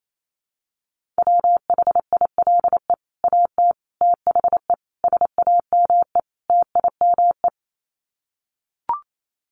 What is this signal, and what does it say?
Morse code, heard as a keyed tone, about 28 wpm, 700 Hz. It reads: WHILE AT THE SAME TIME
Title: Sets of 5 Words 28wpm